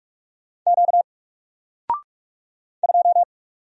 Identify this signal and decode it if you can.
Morse code, heard as a keyed tone, about 44 wpm, 700 Hz. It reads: Q 2
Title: Single Letter-Number 45wpm